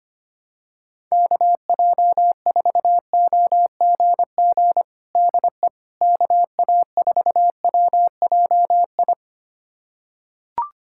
Morse code, heard as a keyed tone, about 25 wpm, 700 Hz. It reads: KJ4OGG DE KA4WJI
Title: USA Callsign DE Callsign - 6 Characters - 25wpm